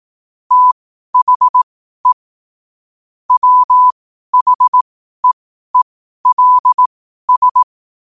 Morse code, heard as a keyed tone, about 18 wpm, spaced as for 12 wpm, 1000 Hz. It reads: THE WHEELS
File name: qotdfiles / 12wpm